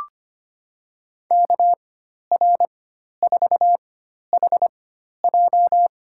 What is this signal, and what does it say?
Morse code, heard as a keyed tone, about 25 wpm, spaced as for 12 wpm, 700 Hz. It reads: KR4HJ
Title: USA Call Signs - 5 Characters - 4x Character Spacing - 25wpm